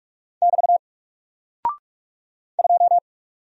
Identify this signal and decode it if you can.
Morse code, heard as a keyed tone, about 44 wpm, 700 Hz. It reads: <BT> 2
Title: Single Character (Letter:Number:Punctuation:Prosign) 45wpm